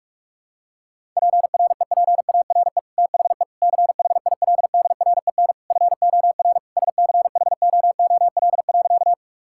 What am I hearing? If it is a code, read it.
Morse code, heard as a keyed tone, about 46 wpm, 700 Hz. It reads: PREPARE THE CHILDREN FOR SCHOOL.
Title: Sentences from Top 1000 Words - Review Part 6 - 45wpm